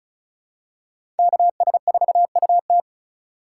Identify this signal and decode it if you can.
Morse code, heard as a keyed tone, about 36 wpm, 700 Hz. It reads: KS4UT
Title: USA Call Signs - 5 Characters - 3x Character Spacing - 35wpm